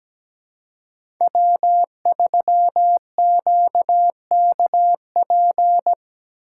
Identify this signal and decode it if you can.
Morse code, heard as a keyed tone, about 17 wpm, 700 Hz. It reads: W3QKP
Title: USA Call Signs - 4x Character Spacing - 17wpm